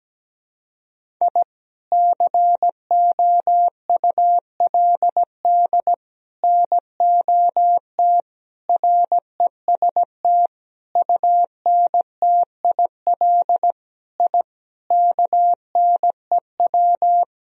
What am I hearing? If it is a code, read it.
Morse code, heard as a keyed tone, about 17 wpm, 700 Hz. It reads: I COULD NOT REST UNTIL I KNEW
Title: Sentences from Top 1000 Words - Review Part 1 - 17wpm